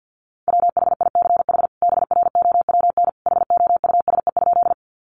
Morse code, heard as a keyed tone, about 50 wpm, 700 Hz. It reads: WHICH BROWN HOUSE?
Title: Sentences from Top 1200 Words 50wpm